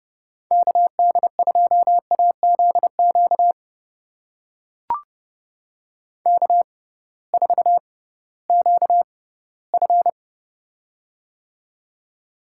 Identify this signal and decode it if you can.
Morse code, heard as a keyed tone, about 29 wpm, spaced as for 17 wpm, 700 Hz. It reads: KD2AZQ K4QF
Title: USA Call Signs - 6x Character Spacing - 30wpm